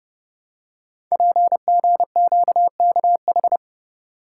Morse code, heard as a keyed tone, about 30 wpm, 700 Hz. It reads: PGQKH
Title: Code Groups - 5 Letters - 30wpm